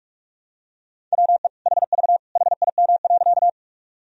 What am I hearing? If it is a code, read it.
Morse code, heard as a keyed tone, about 43 wpm, 700 Hz. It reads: WE HV HIM.
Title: Sentences from Top 100 Words and Abbreviations 45wpm